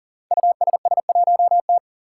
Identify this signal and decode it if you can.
Morse code, heard as a keyed tone, about 43 wpm, 700 Hz. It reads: USS1T
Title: Five Challenging Letter-Number Combinations 40wpm